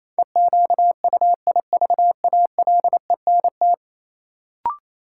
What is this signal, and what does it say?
Morse code, heard as a keyed tone, about 28 wpm, 700 Hz. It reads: EQUIVALENT
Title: Sentences from Top 1600 Words - New Word Drill - 28wpm